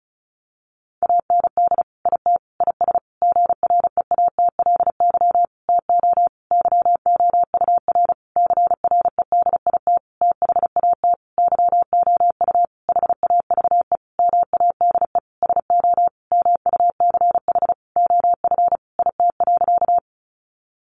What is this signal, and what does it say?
Morse code, heard as a keyed tone, about 34 wpm, 700 Hz. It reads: AND IT IS GREATLY TO YOUR CREDIT THAT YOU HAVE MADE SO MUCH OF IT.
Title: Sentences from Top 1400 Words 35wpm